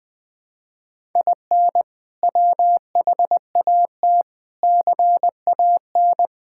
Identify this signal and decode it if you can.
Morse code, heard as a keyed tone, about 20 wpm, 700 Hz. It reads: IN WHAT CAN
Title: Sets of 8 Words - 20wpm